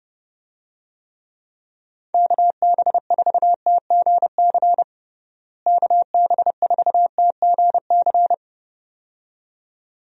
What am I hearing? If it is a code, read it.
Morse code, heard as a keyed tone, about 32 wpm, 700 Hz. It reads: KB4TGC KB4TGC
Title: USA Call Signs - 6 Characters - Repeated 2x - 30wpm